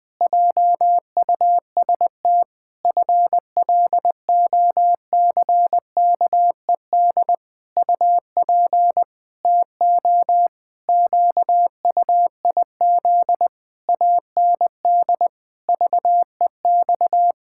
Morse code, heard as a keyed tone, about 20 wpm, 700 Hz. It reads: JUST FLOCKED UP TO QUIZ AND VEX
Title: Pangram Challenge v2 - 20wpm